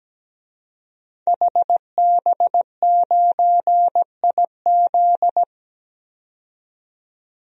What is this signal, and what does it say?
Morse code, heard as a keyed tone, about 17 wpm, 700 Hz. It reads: HB9IZ
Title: DXCC-100 Call Signs - 5 Characters - 17wpm